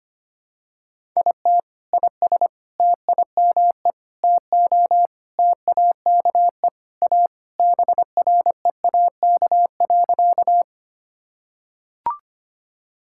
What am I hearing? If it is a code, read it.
Morse code, heard as a keyed tone, about 25 wpm, 700 Hz. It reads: IT IS TIME TO TAKE A BREAK.
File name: practice / 25wpm / Sentences with 4-Word Phrases - 25wpm